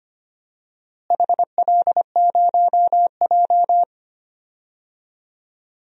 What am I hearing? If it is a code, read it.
Morse code, heard as a keyed tone, about 25 wpm, 700 Hz. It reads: HL0J
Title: DXCC-100 Call Signs - 4 Characters - Repeated 3x - 25wpm